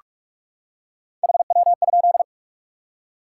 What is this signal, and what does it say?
Morse code, heard as a keyed tone, about 44 wpm, 700 Hz. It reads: HW?
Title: Top 180 QSO Elements 45wpm